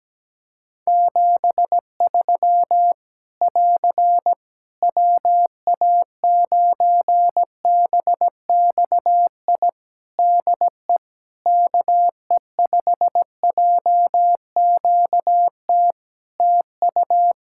Morse code, heard as a keyed tone, about 17 wpm, 700 Hz. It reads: 73 <AR> WA9BXI DE KE5JQT TU
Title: Practice QSO - Exchange 4 - Ending - 17wpm